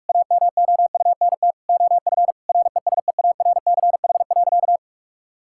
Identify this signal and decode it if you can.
Morse code, heard as a keyed tone, about 46 wpm, 700 Hz. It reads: AMOUNT OF RESEARCH.
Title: Sentences from Top 1000 Words - Review Part 5 - 45wpm